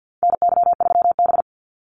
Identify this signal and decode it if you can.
Morse code, heard as a keyed tone, about 50 wpm, 700 Hz. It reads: NY3B
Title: Call Signs - 50wpm